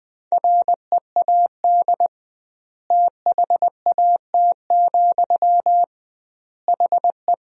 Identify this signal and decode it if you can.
Morse code, heard as a keyed tone, about 21 wpm, 700 Hz. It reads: READ THAT, HE
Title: 2x Word-Spacing - Sentences from Top 200 Words - 20wpm